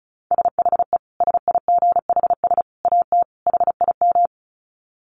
Morse code, heard as a keyed tone, about 34 wpm, 700 Hz. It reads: SHE SIGHS AT HIM
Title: Dit Challenge 35wpm